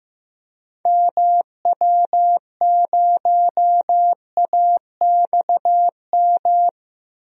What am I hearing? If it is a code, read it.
Morse code, heard as a keyed tone, about 15 wpm, 700 Hz. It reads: MW0AXM
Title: DXCC-100 Call Signs - 6 Characters - Repeated 2x - 15wpm